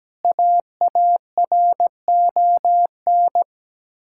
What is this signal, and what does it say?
Morse code, heard as a keyed tone, about 17 wpm, 700 Hz. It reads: AARON
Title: Practice QSO - Exchange 1 - Beginning - 17wpm